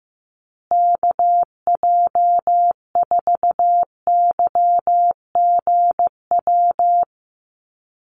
Morse code, heard as a keyed tone, about 15 wpm, 700 Hz. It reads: KJ4YGW
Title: USA Call Signs Repeated 3x 15wpm